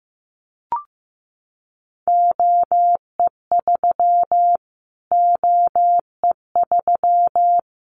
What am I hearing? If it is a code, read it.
Morse code, heard as a keyed tone, about 15 wpm, 700 Hz. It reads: OE3 OE3
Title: Callsign Prefixes for Top 25 DXCC - Repeated 3x - 15wpm